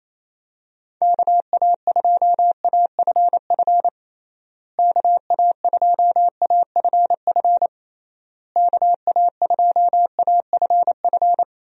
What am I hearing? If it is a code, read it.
Morse code, heard as a keyed tone, about 28 wpm, 700 Hz. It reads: KA2AFF KA2AFF KA2AFF
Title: USA Call Signs - 6 Characters - Repeated 3x - 28wpm